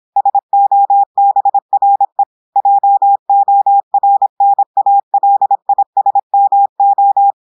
Morse code, heard as a keyed tone, about 26 wpm, 800 Hz. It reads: SOBRE JORNALISMO
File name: audios / portugais / portugais texte en clair 1 vitesse 26 mots minutes